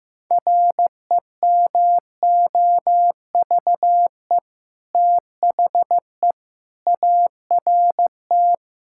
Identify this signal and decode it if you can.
Morse code, heard as a keyed tone, about 15 wpm, 700 Hz. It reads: REMOVE THE ART
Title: Sentences from Top 1000 Words - Review Part 6 - 15wpm